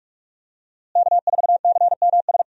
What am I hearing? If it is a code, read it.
Morse code, heard as a keyed tone, about 46 wpm, 700 Hz. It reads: K4CMS
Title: USA Call Signs - 5 Characters - Repeated 2x - 45wpm